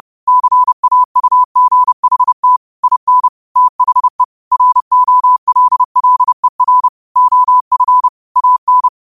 Morse code, heard as a keyed tone, about 30 wpm, 1000 Hz. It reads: CAUGHT IN THE ROLLER OF AN